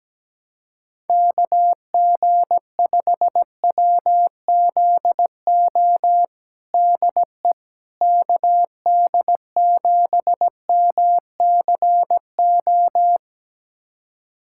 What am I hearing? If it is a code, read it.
Morse code, heard as a keyed tone, about 17 wpm, 700 Hz. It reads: KG5WZO DE KD7MCO
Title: USA Callsign DE Callsign 17wpm